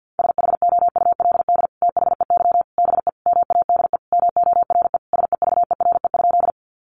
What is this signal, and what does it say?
Morse code, heard as a keyed tone, about 50 wpm, 700 Hz. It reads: SHOULD THEY BE MADE MORE SEVERE?
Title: Sentences from Top 1400 Words 50wpm